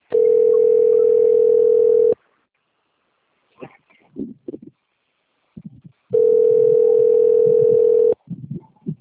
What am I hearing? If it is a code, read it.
Ringback tone (North America), 440 + 480 Hz.
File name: rg-1600-unknown- - - .65820